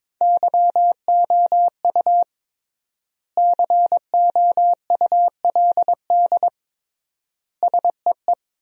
Morse code, heard as a keyed tone, about 22 wpm, 700 Hz. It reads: YOU COULD SEE
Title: 3x Word-Spacing - Sentences from Top 500 Words - 22wpm